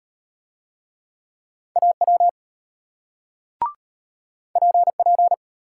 Morse code, heard as a keyed tone, about 34 wpm, 700 Hz. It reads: AW PP